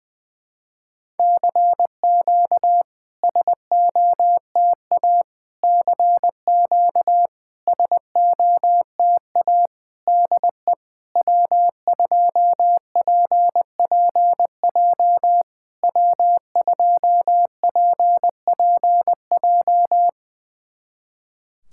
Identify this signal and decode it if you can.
Morse code, heard as a keyed tone, about 20 wpm, 700 Hz. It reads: CQ SOTA CQ SOTA DE W2PPJ W2PPJ
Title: SOTA QSO Practice 20wpm